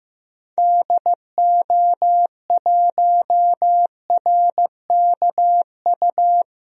Morse code, heard as a keyed tone, about 15 wpm, 700 Hz. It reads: DO1RKU
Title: DXCC-100 Call Signs - 6 Characters - 15wpm